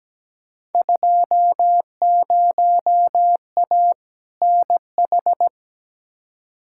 Morse code, heard as a keyed tone, about 17 wpm, 700 Hz. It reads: 20A NH
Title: Field Day Exchanges Encore - 17wpm